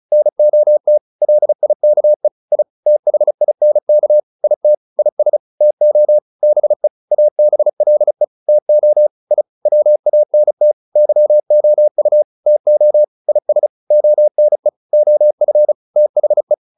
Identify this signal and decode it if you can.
Morse code, heard as a keyed tone, about 35 wpm, 600 Hz. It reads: NOT LIKE I THINK IT IS TO BE ABLE TO I WANT YOU TO IS ONE OF THE
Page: advanced-practice-files